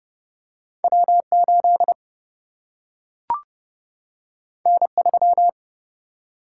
Morse code, heard as a keyed tone, about 30 wpm, 700 Hz. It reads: W 8 N 3
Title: Single-Letter Single-Number 30wpm